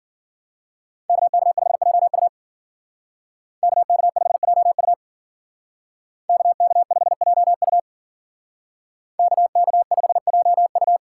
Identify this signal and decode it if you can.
Morse code, heard as a keyed tone, about 50 wpm, 700 Hz. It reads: KK5JU KK5JU KK5JU KK5JU
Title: Speed Racing - USA 2x2 Call Signs 40wpm